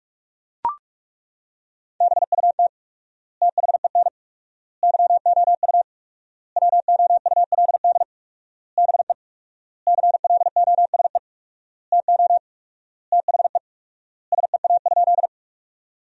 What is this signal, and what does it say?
Morse code, heard as a keyed tone, about 42 wpm, 700 Hz. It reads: BUT THEN YOU WOULD BE CLOSE TO THE SEA?
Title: 4x Word-Spacing - Sentences from Top 300 Words - 45wpm